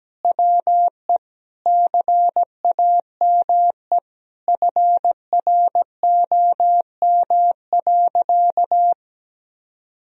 Morse code, heard as a keyed tone, about 17 wpm, 700 Hz. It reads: WE CAME FROM.
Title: Sentences from Top 1000 Words - Review Part 4 - 17wpm